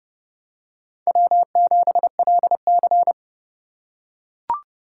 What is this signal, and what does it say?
Morse code, heard as a keyed tone, about 29 wpm, 700 Hz. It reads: W7LC